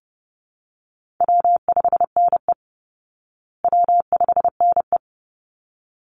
Morse code, heard as a keyed tone, about 30 wpm, 700 Hz. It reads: W5NE W5NE
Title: Easy Call Signs Repeated 2x - 30wpm